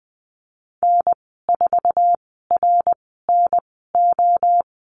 Morse code, heard as a keyed tone, about 20 wpm, spaced as for 14 wpm, 700 Hz. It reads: N4RNO
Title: 055 Lesson 10g - Letter r - Call Signs 20wpm